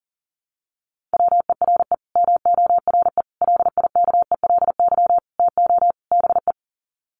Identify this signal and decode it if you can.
Morse code, heard as a keyed tone, about 40 wpm, 700 Hz. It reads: WERE MORE LIKELY TO BE
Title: Sets of 5 Words Encore2 40wpm